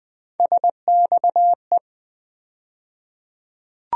Morse code, heard as a keyed tone, about 20 wpm, 700 Hz. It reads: SXE
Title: USA Call Sign Postfixes - 3 Letters 20wpm